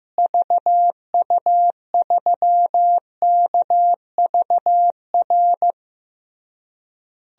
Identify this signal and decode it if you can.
Morse code, heard as a keyed tone, about 15 wpm, 700 Hz. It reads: VU3KVR
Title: DXCC-100 Call Signs - 6 Characters - Repeated 3x - 15wpm